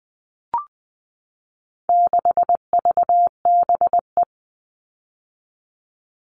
Morse code, heard as a keyed tone, about 19 wpm, 700 Hz. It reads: 6VBE
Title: All Cut Numbers - 4-Digit 20wpm